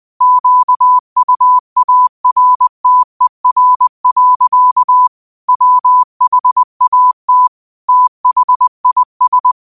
Morse code, heard as a keyed tone, about 20 wpm, 1000 Hz. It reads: QUARTER. WHAT THIS
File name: qotdfiles / 20wpm